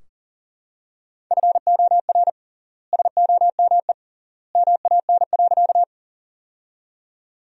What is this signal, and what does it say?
Morse code, heard as a keyed tone, about 40 wpm, 700 Hz. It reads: FOR SOME MAN.
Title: 3x Word-Spacing - Sentences from Top 200 Words - 40wpm